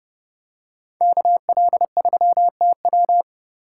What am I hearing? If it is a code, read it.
Morse code, heard as a keyed tone, about 28 wpm, 700 Hz. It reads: KL3TW